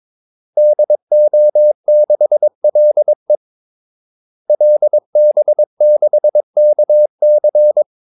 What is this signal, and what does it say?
Morse code, heard as a keyed tone, about 22 wpm, 600 Hz. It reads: DO6LE LB6KC
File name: Callsigns-22wpm